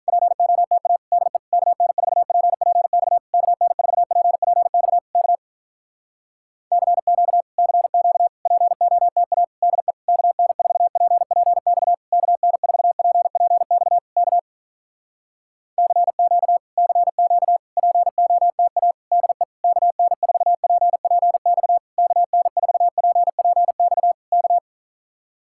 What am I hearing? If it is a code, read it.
Morse code, heard as a keyed tone, about 48 wpm, 700 Hz. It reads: POTA DE KN4PPX KN4PPX K CQ CQ POTA DE KN4PPX KN4PPX K CQ CQ POTA DE KN4PPX KN4PPX K